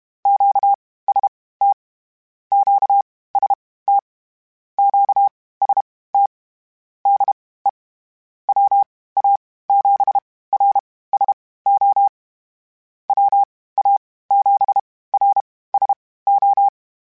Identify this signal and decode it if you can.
Morse code, heard as a keyed tone, about 32 wpm, spaced as for 18 wpm, 800 Hz. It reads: QST QST QST DE WA7RSO WA7RSO